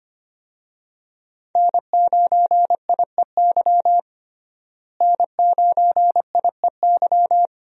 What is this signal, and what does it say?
Morse code, heard as a keyed tone, about 25 wpm, 700 Hz. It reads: N9IEY N9IEY
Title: USA Call Signs - 5 Characters - Repeated 2x - 25wpm